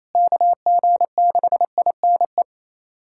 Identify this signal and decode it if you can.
Morse code, heard as a keyed tone, about 28 wpm, 700 Hz. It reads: KG6INE
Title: USA Callsign DE Callsign 28wpm